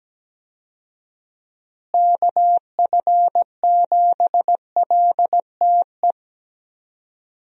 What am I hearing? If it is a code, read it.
Morse code, heard as a keyed tone, about 17 wpm, 700 Hz. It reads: KF7LTE
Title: USA Call Signs - 6 Characters - Repeated 2x - 17wpm